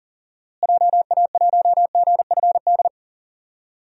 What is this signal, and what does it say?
Morse code, heard as a keyed tone, about 40 wpm, 700 Hz. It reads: JA1GFD
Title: DXCC-100 Call Signs - 6 Characters - Repeated 3x - 40wpm